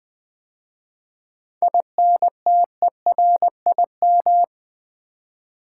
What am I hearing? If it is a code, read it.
Morse code, heard as a keyed tone, about 20 wpm, 700 Hz. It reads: INTERIM
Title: 7 Letter Words - First 3 Letters Given - 20wpm